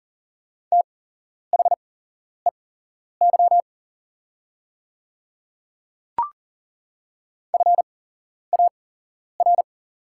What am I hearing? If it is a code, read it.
Morse code, heard as a keyed tone, about 40 wpm, spaced as for 11 wpm, 700 Hz. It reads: THEY FAR
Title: 8x Character Spacing - Top 300 Words - 40wpm